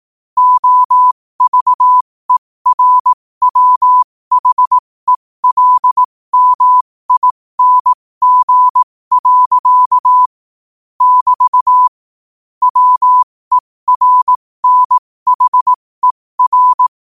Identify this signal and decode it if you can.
Morse code, heard as a keyed tone, about 18 wpm, spaced as for 15 wpm, 1000 Hz. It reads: OVERWHELMING. <BT> WERNHER
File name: qotdfiles / 15wpm